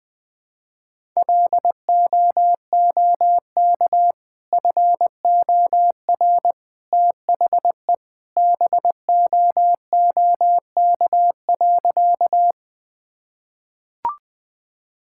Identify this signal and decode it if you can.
Morse code, heard as a keyed tone, about 20 wpm, 700 Hz. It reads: LOOK FOR THE BOOK.
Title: Sentences with 2-Word Phrases - 20wpm